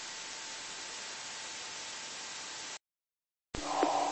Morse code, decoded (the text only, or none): none